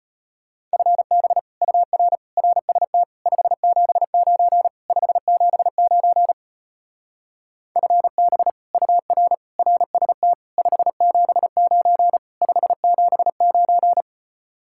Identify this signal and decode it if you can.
Morse code, heard as a keyed tone, about 33 wpm, 700 Hz. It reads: FB UR RST 579 579 FB UR RST 579 579